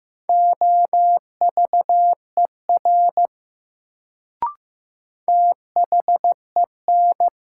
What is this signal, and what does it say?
Morse code, heard as a keyed tone, about 15 wpm, 700 Hz. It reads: OVER THEN
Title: Sentences from Top 100 Words - New Word Drill - 15wpm